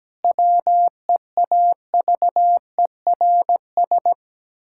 Morse code, heard as a keyed tone, about 17 wpm, 700 Hz. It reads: WEAVERS
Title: Pangram Challenge v2 - 17wpm